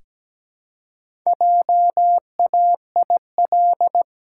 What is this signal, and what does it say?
Morse code, heard as a keyed tone, about 17 wpm, 700 Hz. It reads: JAIL
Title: Pangram Challenge v2 - 17wpm